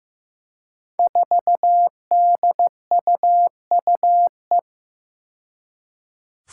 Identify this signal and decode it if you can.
Morse code, heard as a keyed tone, about 15 wpm, 700 Hz. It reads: 4DUUE
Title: Five Challenging Letter-Number Combinations 15wpm